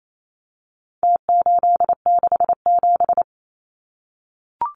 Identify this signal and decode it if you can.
Morse code, heard as a keyed tone, about 28 wpm, 700 Hz. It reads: T867
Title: Numbers - Contest Style - 4-Digit 28wpm